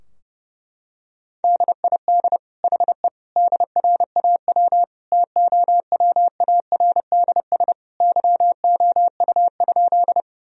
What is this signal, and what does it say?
Morse code, heard as a keyed tone, about 30 wpm, 700 Hz. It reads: DID HE DRAW TOWARDS YOU?
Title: Sentences from Top 1000 Words - Review Part 2 - 30wpm